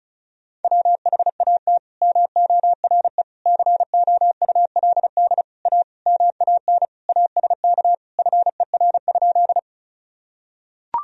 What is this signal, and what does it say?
Morse code, heard as a keyed tone, about 36 wpm, 690 Hz. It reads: WHAT MORE COULD A MAN ASK FER?
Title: Sentences from Top 200 Words and Abbreviations 35wpm